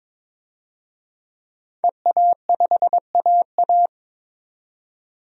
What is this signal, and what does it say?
Morse code, heard as a keyed tone, about 22 wpm, 700 Hz. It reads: EA5AA
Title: DXCC-100 Call Signs - 5 Characters - 22wpm